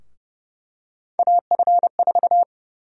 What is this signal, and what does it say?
Morse code, heard as a keyed tone, about 30 wpm, 700 Hz. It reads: AF4
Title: USA Call Sign Prefixes - 30wpm